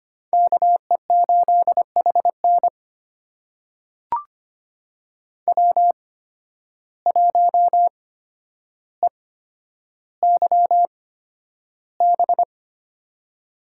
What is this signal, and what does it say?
Morse code, heard as a keyed tone, about 25 wpm, spaced as for 11 wpm, 700 Hz. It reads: KE8HN W1EYB
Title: USA Call Signs - 5 Characters - 8x Character Spacing - 25wpm